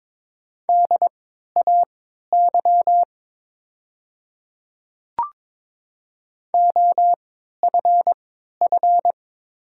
Morse code, heard as a keyed tone, about 22 wpm, spaced as for 12 wpm, 700 Hz. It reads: DAY OFF
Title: 3x Character Spacing - Top 200 Words - 22wpm